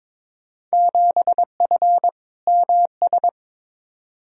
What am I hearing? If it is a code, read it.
Morse code, heard as a keyed tone, about 22 wpm, 700 Hz. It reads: 7F MS
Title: Field Day Exchanges Encore - 22wpm